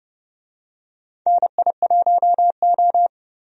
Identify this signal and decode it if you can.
Morse code, heard as a keyed tone, about 30 wpm, 700 Hz. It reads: NI1O
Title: USA Call Signs - 4 Characters - 30wpm